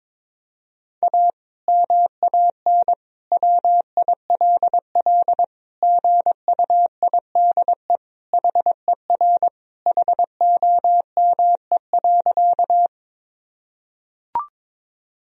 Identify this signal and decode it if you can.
Morse code, heard as a keyed tone, about 22 wpm, 700 Hz. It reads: A MAN WILL GUIDE HER HOME.
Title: Sentences from Top 1000 Words - Review Part 3 - 22wpm